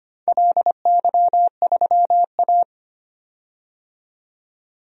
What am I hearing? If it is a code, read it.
Morse code, heard as a keyed tone, about 25 wpm, 700 Hz. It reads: LY3A
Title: DXCC-100 Call Signs - 4 Characters - 25wpm